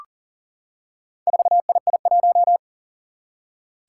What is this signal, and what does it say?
Morse code, heard as a keyed tone, about 38 wpm, 700 Hz. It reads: 4II1
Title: Four Challenging Letter-Number Combinations 40wpm